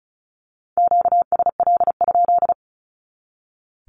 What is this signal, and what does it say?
Morse code, heard as a keyed tone, about 34 wpm, 700 Hz. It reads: QSL?
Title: Q-Codes 35wpm